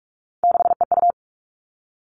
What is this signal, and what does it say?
Morse code, heard as a keyed tone, about 44 wpm, 700 Hz. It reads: 6EU
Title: All Cut Numbers - 3-Digit 45wpm